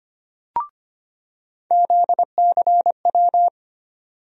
Morse code, heard as a keyed tone, about 25 wpm, 700 Hz. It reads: ZCW
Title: USA Call Sign Postfixes - 3 Letters 25wpm